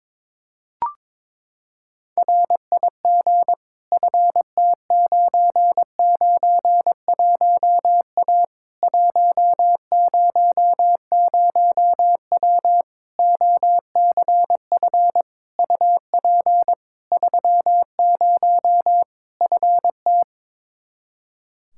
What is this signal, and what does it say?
Morse code, heard as a keyed tone, about 22 wpm, 700 Hz. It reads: RIG FT991A 100W OCF UP 30 FT
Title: Practice QSO - Exchange 2 - Info - 22wpm